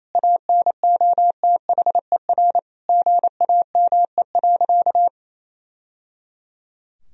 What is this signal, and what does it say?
Morse code, heard as a keyed tone, about 28 wpm, 700 Hz. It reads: ANOTHER GAME.
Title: Sentences from Top 1000 Words - Review Part 4 - 28wpm